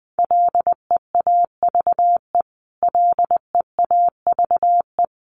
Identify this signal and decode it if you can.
Morse code, heard as a keyed tone, about 20 wpm, 700 Hz. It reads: LEAVE LEAVE
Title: Top 500 Words Repeated 20wpm